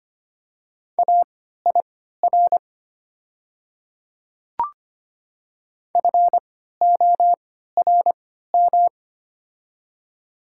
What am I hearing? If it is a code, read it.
Morse code, heard as a keyed tone, about 25 wpm, spaced as for 14 wpm, 700 Hz. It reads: AIR FORM
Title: 3x Character Spacing - Top 200 Words - 25wpm